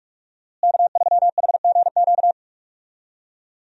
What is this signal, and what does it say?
Morse code, heard as a keyed tone, about 41 wpm, 700 Hz. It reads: K3HGQ
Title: USA Call Signs - 5 Characters - Repeated 3x - 45wpm